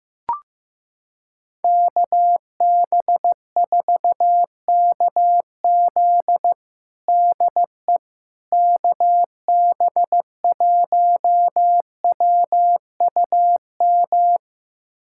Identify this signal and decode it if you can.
Morse code, heard as a keyed tone, about 16 wpm, 700 Hz. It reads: KB4KZ DE KB1WUM
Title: USA Callsign DE Callsign 15wpm